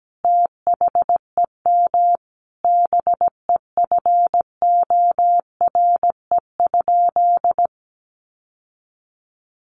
Morse code, heard as a keyed tone, about 17 wpm, 700 Hz. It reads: THEM BEFORE?
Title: Sentences from Top 900 Words 17wpm